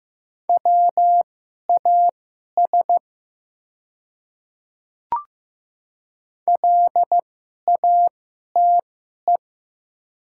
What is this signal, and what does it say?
Morse code, heard as a keyed tone, about 15 wpm, spaced as for 11 wpm, 700 Hz. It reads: WAS LATE
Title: 2x Character Spacing - Top 300 Words - 15wpm